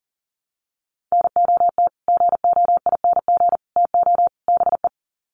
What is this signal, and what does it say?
Morse code, heard as a keyed tone, about 40 wpm, 700 Hz. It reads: NOT GOING TO BE
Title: Sets of 4 Words 40wpm